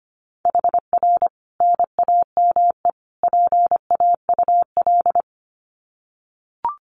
Morse code, heard as a keyed tone, about 25 wpm, 700 Hz. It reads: HR NAME PAUL
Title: Common QSO Phrases 25wpm